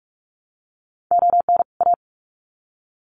Morse code, heard as a keyed tone, about 44 wpm, 700 Hz. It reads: ON A
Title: Sets of 2 Words 45wpm